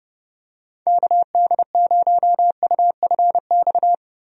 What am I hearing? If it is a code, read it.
Morse code, heard as a keyed tone, about 30 wpm, 700 Hz. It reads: KD0UFX
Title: USA Call Signs - 6 Characters - Repeated 2x - 30wpm